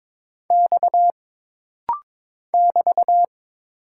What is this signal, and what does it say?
Morse code, heard as a keyed tone, about 22 wpm, 700 Hz. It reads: X <BT>
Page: This rapid-fire timing is geared to push your ICR (Instant Character Recognition) skills.